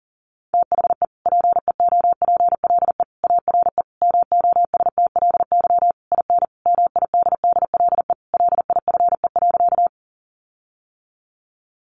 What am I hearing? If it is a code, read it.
Morse code, heard as a keyed tone, about 40 wpm, 700 Hz. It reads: THE PEOPLE ARE MOSTLY IN MIDDLE LIFE.
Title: Sentences from Top 1300 Words 40wpm